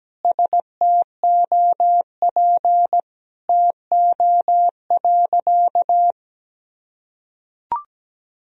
Morse code, heard as a keyed tone, about 17 wpm, 700 Hz. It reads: STOP TO.
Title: Sentences from Top 1000 Words - Review Part 1 - 17wpm